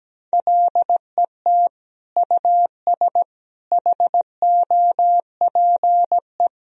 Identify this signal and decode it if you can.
Morse code, heard as a keyed tone, about 17 wpm, 700 Hz. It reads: LET US HOPE
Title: Sentences from Top 1000 Words - Review Part 1 - 17wpm